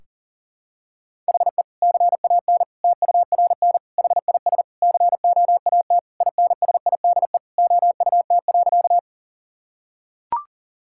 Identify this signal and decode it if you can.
Morse code, heard as a keyed tone, about 40 wpm, 700 Hz. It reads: HE CAN TURN HIS COAT INSIDE OUT.
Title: Sentences from Top 1600 Words 40wpm